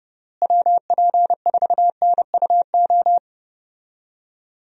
Morse code, heard as a keyed tone, about 29 wpm, 700 Hz. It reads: WP4NUO
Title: USA Call Signs - 6 Characters - Repeated 3x - 30wpm